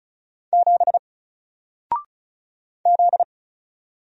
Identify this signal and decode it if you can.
Morse code, heard as a keyed tone, about 33 wpm, 700 Hz. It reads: 7 Z
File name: Single-Letter-Number-35wpm